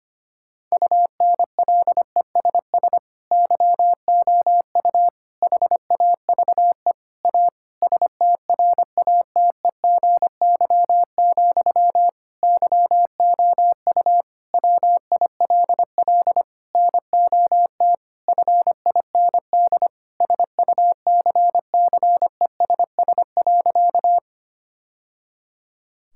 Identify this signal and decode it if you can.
Morse code, heard as a keyed tone, about 25 wpm, 700 Hz. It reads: UNLESS YOU HAVE A STRATEGY, YOU WILL NOT FIND SUCCESS.
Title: Sentences from Top 1000 Words - Review Part 5 - 25wpm